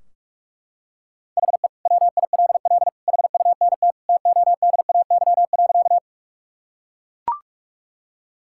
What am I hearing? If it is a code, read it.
Morse code, heard as a keyed tone, about 42 wpm, 700 Hz. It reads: HE WILL HUNT TODAY.
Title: Sentences from Top 1600 Words 45wpm